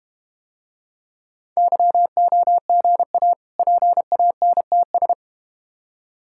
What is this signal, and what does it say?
Morse code, heard as a keyed tone, about 29 wpm, 700 Hz. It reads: YOGA PANTS
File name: practice / 25wpm / Difficult Words - Speed-Racing with Hints - 25wpm